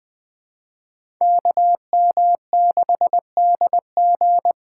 Morse code, heard as a keyed tone, about 20 wpm, 700 Hz. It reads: KM6DG
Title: USA Call Signs - 5 Characters - 3x Character Spacing - 20wpm